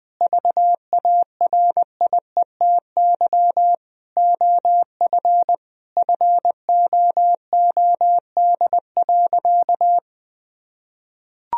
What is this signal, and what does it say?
Morse code, heard as a keyed tone, about 20 wpm, 700 Hz. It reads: VARIETY OF FOOD.
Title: Sentences from Top 1000 Words - Review Part 4 - 20wpm